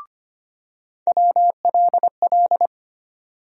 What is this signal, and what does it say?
Morse code, heard as a keyed tone, about 25 wpm, 700 Hz. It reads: WLL
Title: USA Call Sign Postfixes - 3 Letters 25wpm